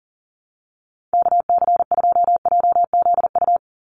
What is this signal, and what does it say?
Morse code, heard as a keyed tone, about 40 wpm, 700 Hz. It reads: KC2JZU
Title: USA Call Signs 40wpm